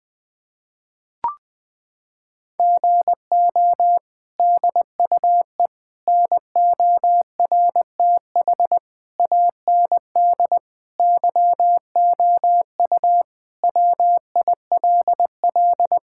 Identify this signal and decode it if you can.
Morse code, heard as a keyed tone, about 20 wpm, 700 Hz. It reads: GO DUE NORTH AND YOU WILL
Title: Sentences from Top 1000 Words - Review Part 6 - 20wpm